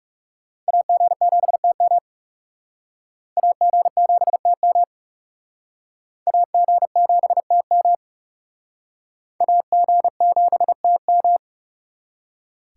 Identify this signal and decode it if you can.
Morse code, heard as a keyed tone, about 37 wpm, 700 Hz. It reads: AG7TM AG7TM AG7TM AG7TM
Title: Speed Racing - USA 2x2 Call Signs 30wpm